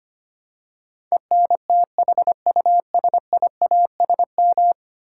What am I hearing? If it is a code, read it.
Morse code, heard as a keyed tone, about 25 wpm, 700 Hz. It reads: ENTHUSIASM
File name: Ten Letter Words - 25wpm